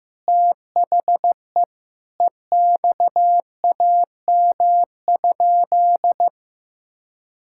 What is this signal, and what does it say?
Morse code, heard as a keyed tone, about 15 wpm, 700 Hz. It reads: THE EXAM?
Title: Sentences from Top 1000 Words - Review Part 5 - 15wpm